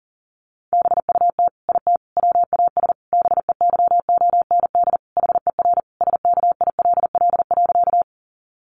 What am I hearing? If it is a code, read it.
Morse code, heard as a keyed tone, about 40 wpm, 700 Hz. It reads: BUT IT WAS BEYOND HER SKILL.
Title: Sentences from Top 1200 Words 40wpm